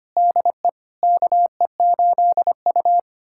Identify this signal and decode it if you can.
Morse code, heard as a keyed tone, about 25 wpm, 700 Hz. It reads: DE KE8U
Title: USA Callsign DE Callsign - 4 Characters - 25wpm